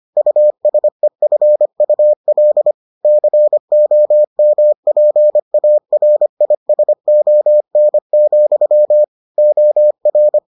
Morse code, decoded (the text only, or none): USEFUL COMPARISON, OR